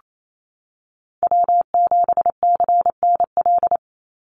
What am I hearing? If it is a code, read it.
Morse code, heard as a keyed tone, about 28 wpm, 700 Hz. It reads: W7CNL
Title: NA SOTA Chaser Callsigns 28wpm